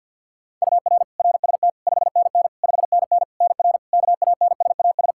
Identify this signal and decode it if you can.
Morse code, heard as a keyed tone, about 51 wpm, 700 Hz. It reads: UR RST 5NN 5NN NR KANSAS
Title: Practice QSO - Exchange 1 - Beginning - 50wpm